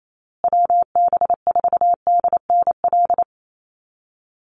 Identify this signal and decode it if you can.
Morse code, heard as a keyed tone, about 28 wpm, 700 Hz. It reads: WB4DNL
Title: Call Signs Repeated 2x - 28wpm